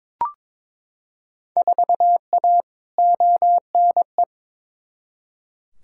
Morse code, heard as a keyed tone, about 22 wpm, 700 Hz. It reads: 4A ONE
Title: Field Day Exchanges Encore - 22wpm